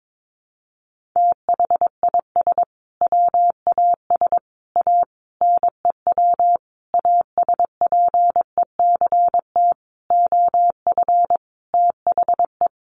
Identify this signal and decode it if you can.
Morse code, heard as a keyed tone, about 22 wpm, 700 Hz. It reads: THIS WAS A NEW ASPECT OF THE
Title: Sentences from Top 1300 Words 22wpm